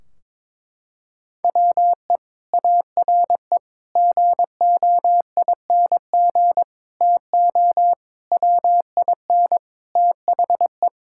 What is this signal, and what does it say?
Morse code, heard as a keyed tone, about 21 wpm, 700 Hz. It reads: WE ARE GOING TO WIN THE
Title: Sentences from Top 1600 Words 22wpm